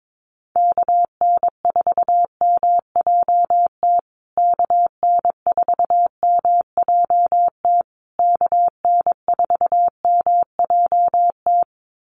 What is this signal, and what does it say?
Morse code, heard as a keyed tone, about 22 wpm, 700 Hz. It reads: KN4MJT KN4MJT KN4MJT
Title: USA Call Signs Repeated 3x 22wpm